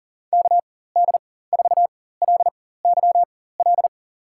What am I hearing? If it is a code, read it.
Morse code, heard as a keyed tone, about 37 wpm, spaced as for 18 wpm, 700 Hz. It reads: KD4LYL
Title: USA Call Signs - 4x Character Spacing - 40wpm